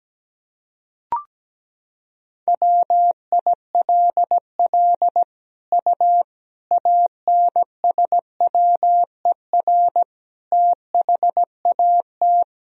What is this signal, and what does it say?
Morse code, heard as a keyed tone, about 17 wpm, 700 Hz. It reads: WILL U ANSWER THAT
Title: Sentences from Top 400 Words and Abbreviations 17wpm